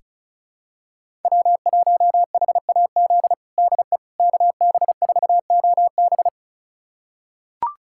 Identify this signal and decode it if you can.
Morse code, heard as a keyed tone, about 36 wpm, 700 Hz. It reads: W1HAZ DE KB4OB
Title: USA Callsign DE Callsign - 5 Characters - 35wpm